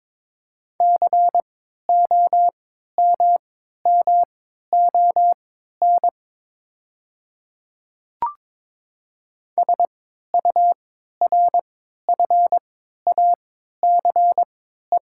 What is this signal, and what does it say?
Morse code, heard as a keyed tone, about 22 wpm, spaced as for 12 wpm, 690 Hz. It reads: COMMON SURFACE
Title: 3x Character Spacing - Top 500 Words - 22wpm